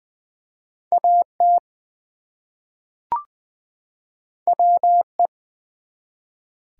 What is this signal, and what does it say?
Morse code, heard as a keyed tone, about 20 wpm, 700 Hz. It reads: AT WE
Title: Two Letter Words - 20wpm